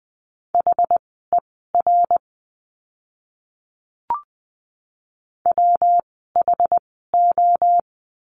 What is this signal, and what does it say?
Morse code, heard as a keyed tone, about 20 wpm, spaced as for 14 wpm, 700 Hz. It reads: HER WHO
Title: 192 Lesson 31f - Letter v - Words 20wpm